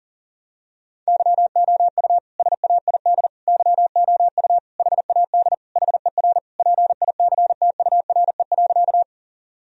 Morse code, heard as a keyed tone, about 40 wpm, 700 Hz. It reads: YOU SAID YOU HAD HER PICTURE.
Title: Sentences from Top 1000 Words - Review Part 1 - 40wpm